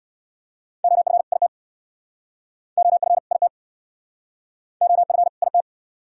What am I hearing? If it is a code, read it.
Morse code, heard as a keyed tone, about 69 wpm, 700 Hz. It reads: OF IT OF IT OF IT